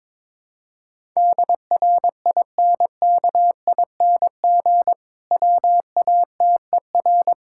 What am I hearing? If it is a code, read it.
Morse code, heard as a keyed tone, about 22 wpm, 700 Hz. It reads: DRINKING WATER
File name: Difficult Sentences with Hints - 22wpm